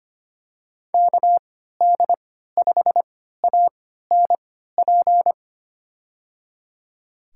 Morse code, heard as a keyed tone, about 25 wpm, spaced as for 14 wpm, 700 Hz. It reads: KD5ANP
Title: USA Call Signs - 6 Characters - 3x Character Spacing - 25wpm